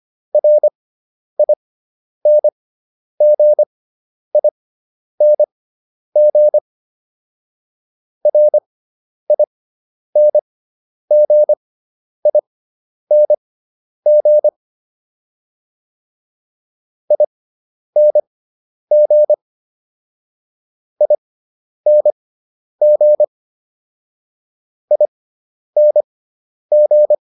Morse code, heard as a keyed tone, about 25 wpm, spaced as for 10 wpm, 600 Hz. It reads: RINGING RINGING ING ING ING